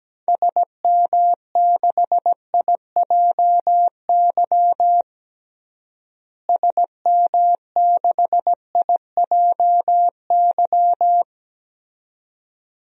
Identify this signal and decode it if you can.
Morse code, heard as a keyed tone, about 17 wpm, 700 Hz. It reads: SM6IJY SM6IJY
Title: DXCC-100 Call Signs - 6 Characters - Repeated 2x - 17wpm